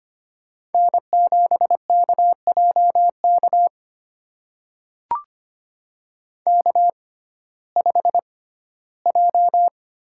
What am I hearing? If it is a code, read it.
Morse code, heard as a keyed tone, about 25 wpm, spaced as for 15 wpm, 700 Hz. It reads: N7KJK K5J
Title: USA Call Signs - 5 Characters - 6x Character Spacing - 25wpm